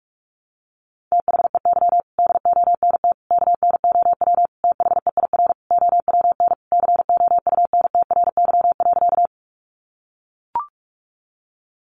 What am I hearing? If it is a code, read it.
Morse code, heard as a keyed tone, about 44 wpm, 700 Hz. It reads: THEY DONT KNOW THEIR OWN COUNTRY.
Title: Sentences from Top 300 Words 45wpm